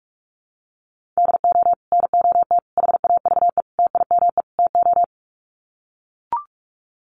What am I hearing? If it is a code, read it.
Morse code, heard as a keyed tone, about 44 wpm, 700 Hz. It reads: DO NOT HAVE TIME TO
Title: Sets of 5 Words 45wpm